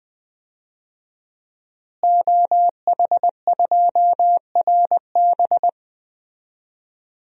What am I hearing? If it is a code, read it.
Morse code, heard as a keyed tone, about 21 wpm, 700 Hz. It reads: OH2RB
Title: DXCC-100 Call Signs - 5 Characters - Repeated 2x - 20wpm